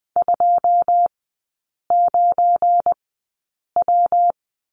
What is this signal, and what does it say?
Morse code, heard as a keyed tone, about 20 wpm, spaced as for 8 wpm, 700 Hz. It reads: 29W
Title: 178 Lesson 29e - Slash - Three Characters 20wpm